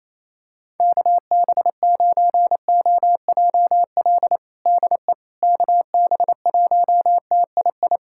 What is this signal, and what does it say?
Morse code, heard as a keyed tone, about 28 wpm, 700 Hz. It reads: KB9OJL DE KB1TII
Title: USA Callsign DE Callsign 28wpm